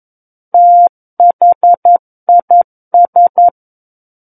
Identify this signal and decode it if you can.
Morse code, heard as a keyed tone, about 11 wpm, 700 Hz. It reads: THIS
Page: War of the Worlds - 02-Chapter 2 - 11 WPM